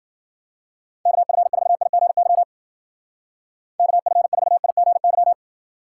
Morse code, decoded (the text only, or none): KF4IGY KF4IGY